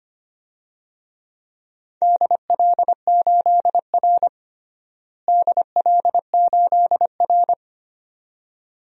Morse code, heard as a keyed tone, about 25 wpm, 700 Hz. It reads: DL8R DL8R
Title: DXCC-100 Call Signs - 4 Characters - Repeated 2x - 25wpm